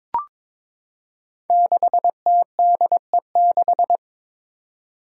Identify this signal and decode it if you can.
Morse code, heard as a keyed tone, about 22 wpm, 700 Hz. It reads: 6TDE6
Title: Five Challenging Letter-Number Combinations 22wpm